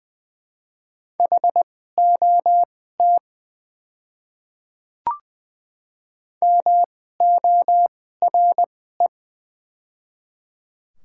Morse code, heard as a keyed tone, about 20 wpm, spaced as for 15 wpm, 700 Hz. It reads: HOT MORE
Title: 2x Character Spacing - Top 100 Words - 20wpm